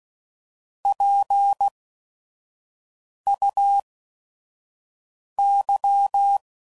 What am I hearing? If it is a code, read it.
Morse code, heard as a keyed tone, about 16 wpm, 800 Hz. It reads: P U Y